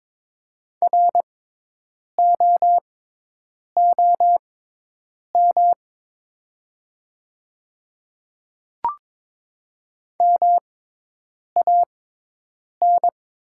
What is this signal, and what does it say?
Morse code, heard as a keyed tone, about 22 wpm, spaced as for 8 wpm, 700 Hz. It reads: ROOM MAN
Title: 6x Character Spacing - Top 300 Words - 22wpm